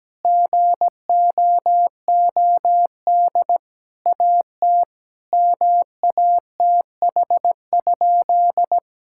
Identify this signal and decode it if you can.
Morse code, heard as a keyed tone, about 17 wpm, 700 Hz. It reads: GOOD AT MATH?
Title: Sentences from Top 1000 Words - Review Part 6 - 17wpm